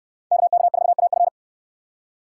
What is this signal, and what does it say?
Morse code, heard as a keyed tone, about 68 wpm, 700 Hz. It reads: KK4AF